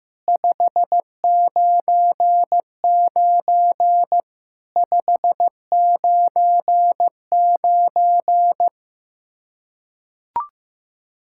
Morse code, heard as a keyed tone, about 15 wpm, 700 Hz. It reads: 599 599
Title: Practice QSO - Exchange 1 - Beginning - 15wpm